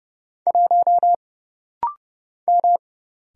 Morse code, heard as a keyed tone, about 31 wpm, 700 Hz. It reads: 1 M
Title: Single Letter-Number Alt - Rapid-Fire 30wpm